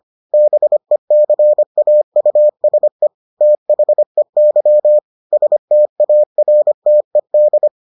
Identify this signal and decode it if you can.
Morse code, heard as a keyed tone, about 25 wpm, 600 Hz. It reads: BECAUSE THEY STARTED